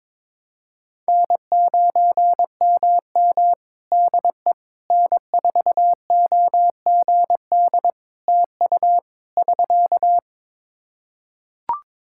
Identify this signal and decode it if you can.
Morse code, heard as a keyed tone, about 22 wpm, 700 Hz. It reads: N9MM DE N4OGD TU <SK>
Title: K1USN SST QSO Practice 22wpm